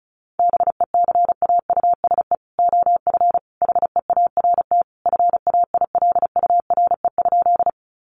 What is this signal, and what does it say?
Morse code, heard as a keyed tone, about 34 wpm, 700 Hz. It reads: BECAUSE OF HEART FAILURE?
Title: Sentences from Top 1200 Words 35wpm